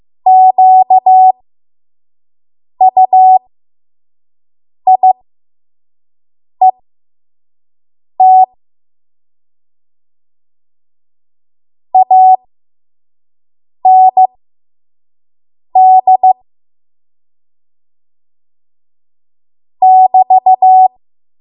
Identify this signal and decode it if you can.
Morse code, heard as a keyed tone, about 15 wpm, spaced as for 5 wpm, 750 Hz. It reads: QUIET AND <BT>
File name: Archive / 5 WPM / 05WPM